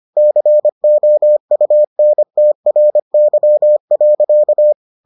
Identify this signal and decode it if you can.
Morse code, heard as a keyed tone, about 25 wpm, 600 Hz. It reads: COUNTRY.